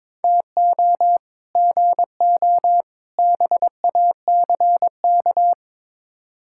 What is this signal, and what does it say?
Morse code, heard as a keyed tone, about 22 wpm, 700 Hz. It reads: TO GO BACK
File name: practice / 22wpm / Build A Phrase - 5-Word Phrases - 22wpm